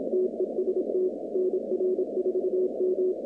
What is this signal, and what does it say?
Morse code, heard as a keyed tone, about 27 wpm, 370 Hz, with noise. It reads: TEST NR4M
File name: 2019-cw-wpx-kt5j-k5pi